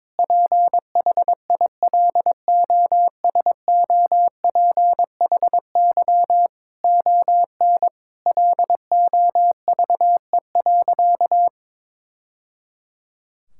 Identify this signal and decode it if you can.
Morse code, heard as a keyed tone, about 22 wpm, 700 Hz. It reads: PHILOSOPHY ON LOVE.
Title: Sentences from Top 1000 Words - Review Part 6 - 22wpm